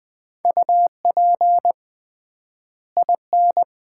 Morse code, heard as a keyed tone, about 21 wpm, 700 Hz. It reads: UP IN
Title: 3x Word-Spacing - Sets of 2 Words - 20wpm